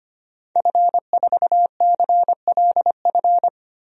Morse code, heard as a keyed tone, about 25 wpm, 700 Hz. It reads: F4CLF
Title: DXCC-100 Call Signs - 5 Characters - 25wpm